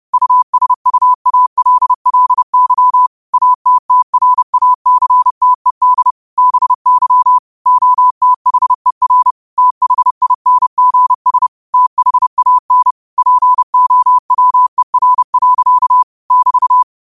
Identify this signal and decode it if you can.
Morse code, heard as a keyed tone, about 30 wpm, 1000 Hz. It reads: USUALLY ATTRACTED BY OTHER THINGS THAN POWER. <BT>
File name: qotdfiles / 30wpm